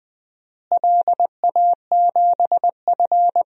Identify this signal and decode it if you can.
Morse code, heard as a keyed tone, about 20 wpm, 700 Hz. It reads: LA7F
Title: DXCC-100 Call Signs - 4 Characters - Repeated 3x - 20wpm